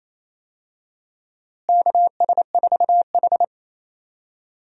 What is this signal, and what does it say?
Morse code, heard as a keyed tone, about 28 wpm, 700 Hz. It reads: KS4H
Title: USA Call Signs - 4 Characters - Repeated 3x - 28wpm